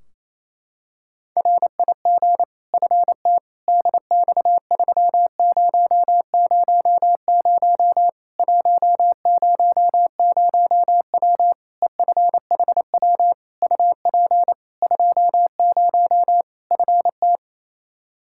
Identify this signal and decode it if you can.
Morse code, heard as a keyed tone, about 28 wpm, 700 Hz. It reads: RIG FT DX3000 100W EFHW UP 20 FT
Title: K1USN SST QSO Practice 28wpm